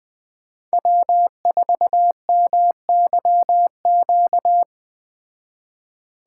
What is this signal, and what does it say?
Morse code, heard as a keyed tone, about 20 wpm, 700 Hz. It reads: W4MYQ
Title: Speed Racing - USA 1x3 Call Signs 17wpm